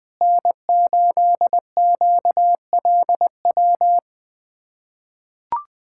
Morse code, heard as a keyed tone, about 20 wpm, 700 Hz. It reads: N8QLW
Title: USA Callsign DE Callsign - 5 Characters - 20wpm